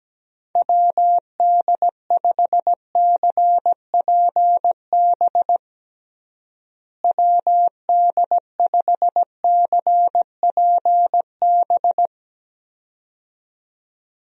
Morse code, heard as a keyed tone, about 17 wpm, 700 Hz. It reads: WD5CPB WD5CPB
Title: USA Call Signs - 6 Characters - Repeated 2x - 17wpm